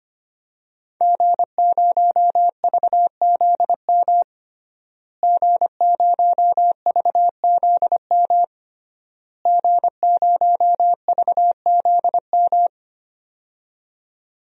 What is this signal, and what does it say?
Morse code, heard as a keyed tone, about 25 wpm, 700 Hz. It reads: G0VZM G0VZM G0VZM
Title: DXCC-100 Call Signs - 5 Characters - Repeated 3x - 25wpm